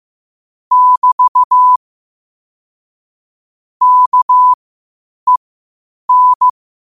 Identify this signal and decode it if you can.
Morse code, heard as a keyed tone, about 15 wpm, spaced as for 8 wpm, 1000 Hz. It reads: <BT> KEN
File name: qotdfiles / 8wpm